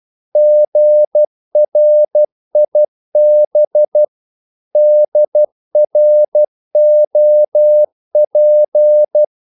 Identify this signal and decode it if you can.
Morse code, heard as a keyed tone, about 12 wpm, 600 Hz. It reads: GRIB DROP
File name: Korte ord DK 12wpm